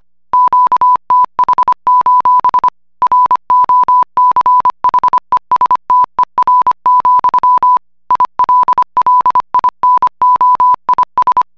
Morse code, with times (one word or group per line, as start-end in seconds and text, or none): 0.3-2.7 QTH:
3.0-7.8 ROCHESTER,
8.1-11.4 ILLINOIS